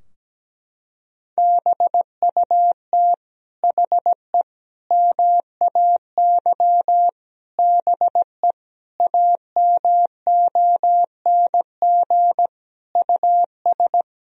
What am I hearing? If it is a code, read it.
Morse code, heard as a keyed tone, about 17 wpm, 700 Hz. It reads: BUT HE MAY BE AMONG US
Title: Sentences from Top 1000 Words - Review Part 3 - 17wpm